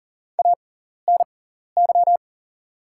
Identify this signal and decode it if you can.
Morse code, heard as a keyed tone, about 40 wpm, spaced as for 14 wpm, 700 Hz. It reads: ANY
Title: 6x Character Spacing - Top 200 Words - 40wpm